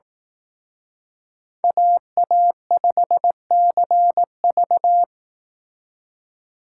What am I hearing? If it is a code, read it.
Morse code, heard as a keyed tone, about 18 wpm, 700 Hz. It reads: AA5CV
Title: Speed Racing - USA 2x2 Call Signs 15wpm